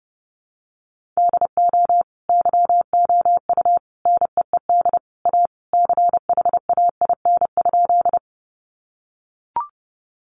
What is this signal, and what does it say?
Morse code, heard as a keyed tone, about 30 wpm, 700 Hz. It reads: DO YOU NEED A CHAIN?
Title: Sentences from Top 1500 Words 30wpm